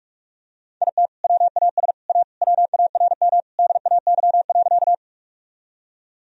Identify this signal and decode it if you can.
Morse code, heard as a keyed tone, about 43 wpm, 700 Hz. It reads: IT WAS A WARM DAY.
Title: Sentences from Top 1000 Words - Review Part 2 - 45wpm